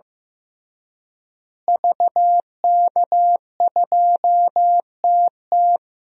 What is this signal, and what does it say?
Morse code, heard as a keyed tone, about 15 wpm, 700 Hz. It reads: VK2TT
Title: DXCC-100 Call Signs - 5 Characters - Repeated 3x - 15wpm